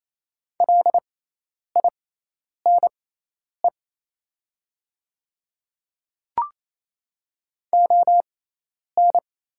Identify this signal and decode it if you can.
Morse code, heard as a keyed tone, about 28 wpm, spaced as for 10 wpm, 700 Hz. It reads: LINE ON
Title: 6x Character Spacing - Top 500 Words - 28wpm